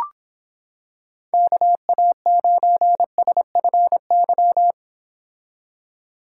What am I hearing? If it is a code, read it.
Morse code, heard as a keyed tone, about 26 wpm, 700 Hz. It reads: KA9SFY